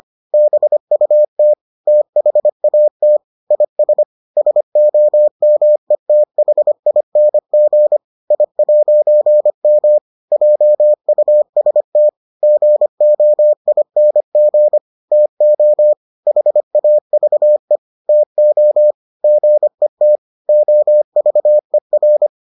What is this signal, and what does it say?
Morse code, heard as a keyed tone, about 25 wpm, 600 Hz. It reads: BUT THAT IS SOMETHING I'M JUST GOING TO HAVE TO GET OVER